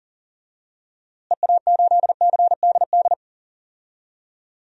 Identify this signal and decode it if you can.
Morse code, heard as a keyed tone, about 37 wpm, 700 Hz. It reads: EA8CDD
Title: DXCC-100 Call Signs - 6 Characters - Repeated 3x - 40wpm